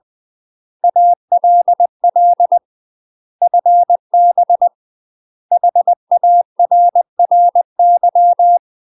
Morse code, heard as a keyed tone, about 20 wpm, 700 Hz. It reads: ALL FB HARRY
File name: UK MASTER QSOs TRAINING/20WPM